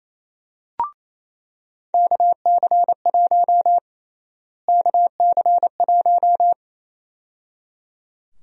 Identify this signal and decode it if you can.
Morse code, heard as a keyed tone, about 28 wpm, 700 Hz. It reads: KC1 KC1
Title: USA Call Sign Prefixes - Repeated 2x - 28wpm